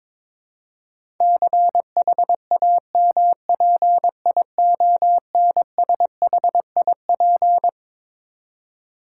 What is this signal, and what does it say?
Morse code, heard as a keyed tone, about 22 wpm, 700 Hz. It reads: CHAMPIONSHIP
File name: Twelve Letter Words - 22wpm